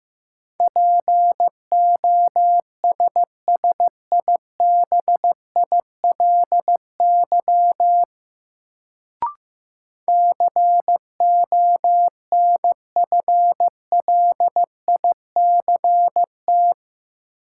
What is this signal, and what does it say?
Morse code, heard as a keyed tone, about 15 wpm, 700 Hz. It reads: POSSIBILY CONFLICT
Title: Sentences from Top 1300 Words - New Word Drill - 15wpm